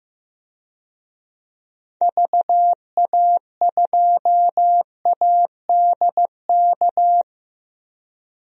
Morse code, heard as a keyed tone, about 15 wpm, 700 Hz. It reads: VA2ADK
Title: DXCC-100 Call Signs - 6 Characters - 15wpm